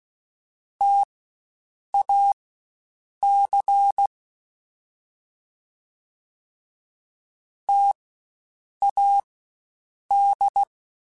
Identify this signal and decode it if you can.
Morse code, heard as a keyed tone, about 16 wpm, spaced as for 7 wpm, 800 Hz. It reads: TAC TAD
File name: Spaced Alphas / Spaced Alphas mp3s / Ts sp @15